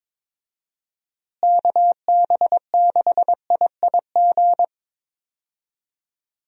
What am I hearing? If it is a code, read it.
Morse code, heard as a keyed tone, about 22 wpm, 700 Hz. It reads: KB6IIG
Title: USA Call Signs - 6 Characters - Repeated 2x - 22wpm